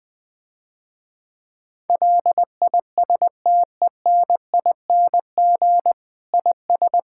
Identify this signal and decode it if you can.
Morse code, heard as a keyed tone, about 20 wpm, 700 Hz. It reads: LISTENING IS
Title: Difficult Sentences with Hints - 20wpm